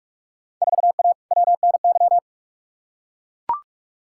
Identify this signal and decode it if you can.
Morse code, heard as a keyed tone, about 41 wpm, 700 Hz. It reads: 4A WNY
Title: Field Day Exchanges Encore - 45wpm